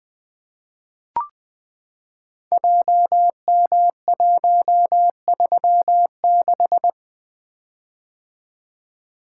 Morse code, heard as a keyed tone, about 20 wpm, 700 Hz. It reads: JM136
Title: Code Groups - 5 Letter:Numbers - 20wpm